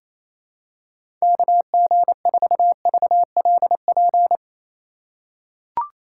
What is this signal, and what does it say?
Morse code, heard as a keyed tone, about 28 wpm, 700 Hz. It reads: KG4VLP
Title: USA Call Signs - 6 Characters - 4x Character Spacing - 28wpm